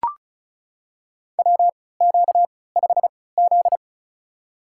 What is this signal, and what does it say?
Morse code, heard as a keyed tone, about 36 wpm, spaced as for 20 wpm, 700 Hz. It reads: WQ5Z
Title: USA Call Signs - 3x Character Spacing - 35wpm